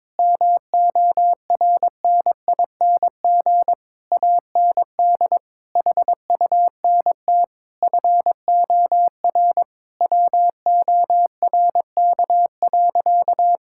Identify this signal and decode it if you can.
Morse code, heard as a keyed tone, about 22 wpm, 700 Hz. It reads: MORNING AND HUNT FOR WORK.
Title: Sentences from Top 1600 Words 22wpm